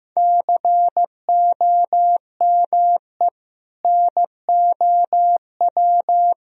Morse code, heard as a keyed tone, about 15 wpm, 700 Hz. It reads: COME NOW
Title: Sentences from Top 1000 Words - Review Part 4 - 15wpm